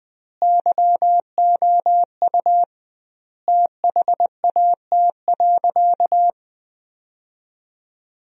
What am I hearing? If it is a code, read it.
Morse code, heard as a keyed tone, about 20 wpm, 700 Hz. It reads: YOU THAT.
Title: 2x Word-Spacing - Sentences from Top 300 Words - 20wpm